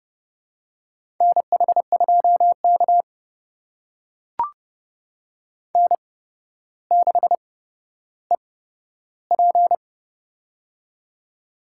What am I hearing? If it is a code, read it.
Morse code, heard as a keyed tone, about 30 wpm, spaced as for 13 wpm, 700 Hz. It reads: NH2K N6EP
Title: USA Call Signs - 4 Characters - 8x Character Spacing - 30wpm